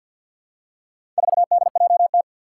Morse code, heard as a keyed tone, about 51 wpm, 700 Hz. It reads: 4DJT
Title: Four Challenging Letter-Number Combinations 50wpm